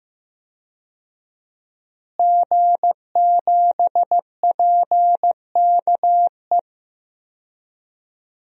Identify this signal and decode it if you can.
Morse code, heard as a keyed tone, about 15 wpm, 700 Hz. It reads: G7PKE
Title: DXCC-100 Call Signs - 5 Characters - Repeated 3x - 15wpm